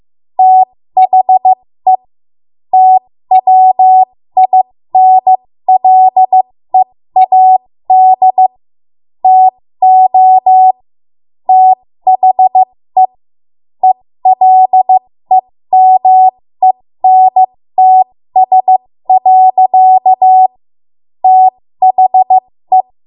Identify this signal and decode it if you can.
Morse code, heard as a keyed tone, about 15 wpm, spaced as for 13 wpm, 750 Hz. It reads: THE TWINLEAD TO THE ELEMENTS. THE
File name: Morse / 13WPM